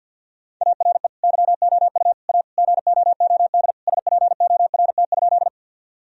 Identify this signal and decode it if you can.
Morse code, heard as a keyed tone, about 50 wpm, 700 Hz. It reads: ARE YOU A GOOD SPORT?
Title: Sentences from Top 1000 Words - Review Part 4 - 50wpm